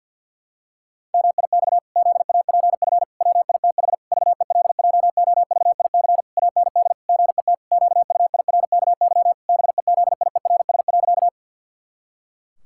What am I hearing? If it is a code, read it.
Morse code, heard as a keyed tone, about 51 wpm, 700 Hz. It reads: MIX ZAPF WITH VELJOVIC AND GET QUIRKY BEZIERS.
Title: Pangram Challenge v2 - 50wpm